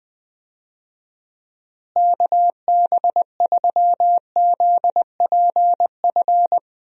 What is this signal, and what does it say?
Morse code, heard as a keyed tone, about 20 wpm, 700 Hz. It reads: KB3ZPF
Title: USA Call Signs - 6 Characters - Repeated 3x - 20wpm